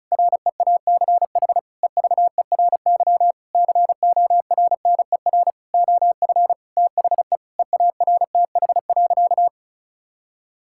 Morse code, heard as a keyed tone, about 34 wpm, 700 Hz. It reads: REACH EVERY CORNER OF THE EARTH.
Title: Sentences from Top 1600 Words 35wpm